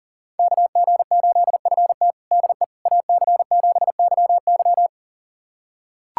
Morse code, heard as a keyed tone, about 37 wpm, 700 Hz. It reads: KG8FT DE AC7YY
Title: USA Callsign DE Callsign - 5 Characters - 40wpm